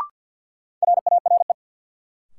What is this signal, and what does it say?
Morse code, heard as a keyed tone, about 46 wpm, 700 Hz. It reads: RARE
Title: Sentences from Top 1100 Words - New Word Drill Alt - 50wpm